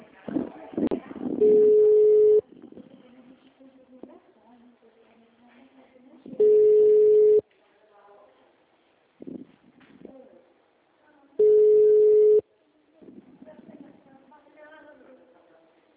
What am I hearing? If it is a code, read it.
Ringback tone (Europe, 425 Hz), with speech.
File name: out- -2015- - - .27222